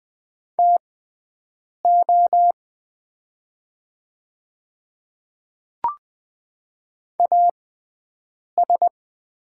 Morse code, heard as a keyed tone, about 20 wpm, spaced as for 7 wpm, 700 Hz. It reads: TO AS
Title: 6x Character Spacing - Top 300 Words - 20wpm